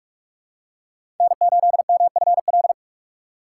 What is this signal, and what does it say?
Morse code, heard as a keyed tone, about 46 wpm, 700 Hz. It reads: N8MFL
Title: USA Call Signs - 5 Characters - 3x Character Spacing - 45wpm